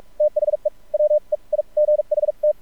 Morse code, heard as a keyed tone, about 46 wpm, spaced as for 36 wpm, 600 Hz. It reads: THE WEIGHT
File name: archived / CRSnet